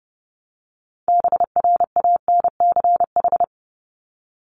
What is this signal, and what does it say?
Morse code, heard as a keyed tone, about 30 wpm, 700 Hz. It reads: BRANCH
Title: Top 1751-2000 Words 30wpm